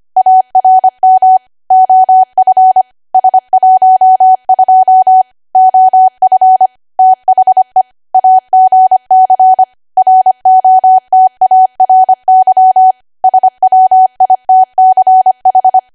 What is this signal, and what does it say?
Morse code, heard as a keyed tone, about 25 wpm, 750 Hz. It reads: ARM OF S12 OF THE AGC ROTARY SWITCH
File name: Archive / 25 WPM / 25WPM